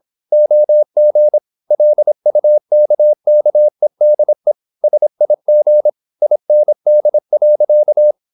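Morse code, heard as a keyed tone, about 26 wpm, 600 Hz. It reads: OG LUKKEDE SIG IND.
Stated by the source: High-speed (QRQ) Morse at 26 wpm.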